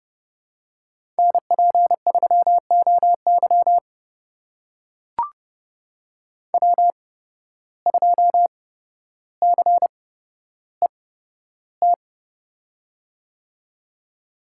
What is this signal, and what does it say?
Morse code, heard as a keyed tone, about 30 wpm, spaced as for 13 wpm, 700 Hz. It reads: NP3OY W2CET
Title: USA Call Signs - 8x Character Spacing - 30wpm